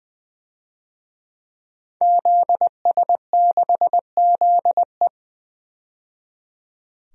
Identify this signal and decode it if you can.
Morse code, heard as a keyed tone, about 20 wpm, 700 Hz. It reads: ZS6ZE
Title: DXCC-100 Call Signs - 5 Characters - 20wpm